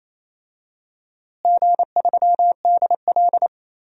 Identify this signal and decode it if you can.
Morse code, heard as a keyed tone, about 28 wpm, 700 Hz. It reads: G3DL
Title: DXCC-100 Call Signs - 4 Characters - Repeated 3x - 28wpm